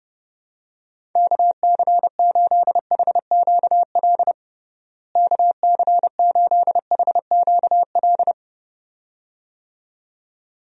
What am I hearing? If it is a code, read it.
Morse code, heard as a keyed tone, about 32 wpm, 700 Hz. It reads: KC8HQL KC8HQL
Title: USA Call Signs - 6 Characters - Repeated 2x - 30wpm